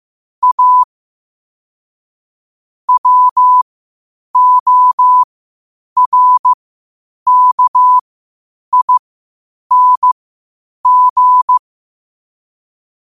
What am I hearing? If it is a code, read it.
Morse code, heard as a keyed tone, about 15 wpm, spaced as for 8 wpm, 1000 Hz. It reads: A WORKING
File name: qotdfiles / 8wpm